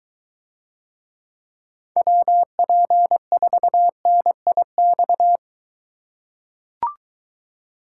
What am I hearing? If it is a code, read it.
Morse code, heard as a keyed tone, about 22 wpm, 700 Hz. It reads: WP4NIX